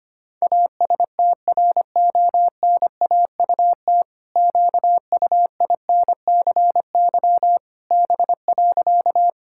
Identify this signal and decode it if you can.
Morse code, heard as a keyed tone, about 25 wpm, 700 Hz. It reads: ASTRONAUT QUINCY B.
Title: Pangram Challenge v2 - 25wpm